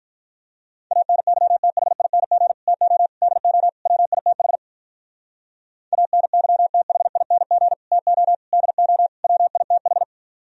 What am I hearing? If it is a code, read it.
Morse code, heard as a keyed tone, about 46 wpm, 700 Hz. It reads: ANYTHING TO DO WITH ANYTHING TO DO WITH